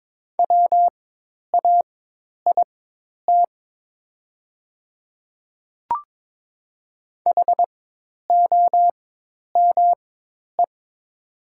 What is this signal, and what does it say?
Morse code, heard as a keyed tone, about 22 wpm, spaced as for 10 wpm, 700 Hz. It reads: WAIT HOME
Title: 4x Character Spacing - Top 500 Words - 22wpm